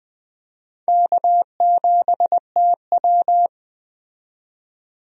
Morse code, heard as a keyed tone, about 20 wpm, 700 Hz. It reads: K7TW
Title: USA Call Signs - 4 Characters - 20wpm